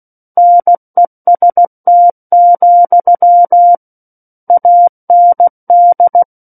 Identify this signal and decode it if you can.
Morse code, heard as a keyed tone, about 16 wpm, 700 Hz. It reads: NEST, AND
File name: Book One / 07-Chapter 7 - 16 WPM